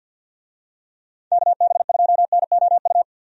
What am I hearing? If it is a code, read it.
Morse code, heard as a keyed tone, about 53 wpm, 700 Hz. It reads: KB2NOU
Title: USA Call Signs - 6 Characters - 50wpm